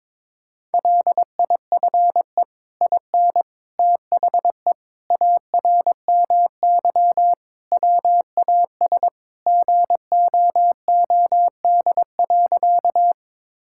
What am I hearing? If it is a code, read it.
Morse code, heard as a keyed tone, about 22 wpm, 700 Hz. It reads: LIFE IN THE ARMY WAS GOOD.
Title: Sentences from Top 1000 Words - Review Part 5 - 22wpm